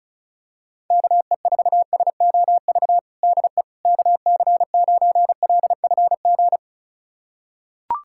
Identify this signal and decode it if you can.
Morse code, heard as a keyed tone, about 34 wpm, 700 Hz. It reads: KE4SOV DE KC9LFG
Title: USA Callsign DE Callsign - 6 Characters - 35wpm